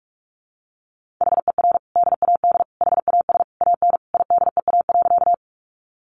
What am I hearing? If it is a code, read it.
Morse code, heard as a keyed tone, about 44 wpm, 700 Hz. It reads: HER DAD HAS AN IDEA.
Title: Easy Phrases 45wpm